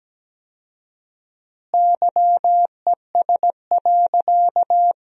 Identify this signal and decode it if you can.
Morse code, heard as a keyed tone, about 17 wpm, 700 Hz. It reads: YES.
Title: 4x Word-Spacing - Sentences from Top 500 Words - 17wpm